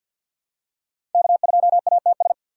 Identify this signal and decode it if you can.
Morse code, heard as a keyed tone, about 50 wpm, 700 Hz. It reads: K2ATS
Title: USA Call Signs - 3x Character Spacing - 50wpm